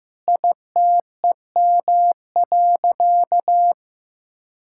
Morse code, heard as a keyed tone, about 15 wpm, 700 Hz. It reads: ITEM.
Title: Sentences from Top 1000 Words - Review Part 6 - 15wpm